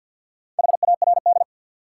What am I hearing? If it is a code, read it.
Morse code, heard as a keyed tone, about 53 wpm, 700 Hz. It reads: HARD
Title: Sentences from Top 300 Words - New Word Drill - 50wpm